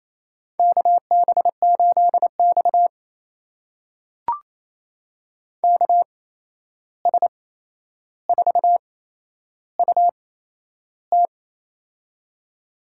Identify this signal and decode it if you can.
Morse code, heard as a keyed tone, about 28 wpm, spaced as for 11 wpm, 700 Hz. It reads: KB8X KS4UT
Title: USA Call Signs - 8x Character Spacing - 28wpm